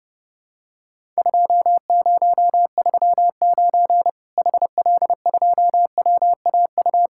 Morse code, heard as a keyed tone, about 29 wpm, 700 Hz. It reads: 2039 HL2WAU
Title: Worked All Europe DX Contest - QTC Exchange - 30wpm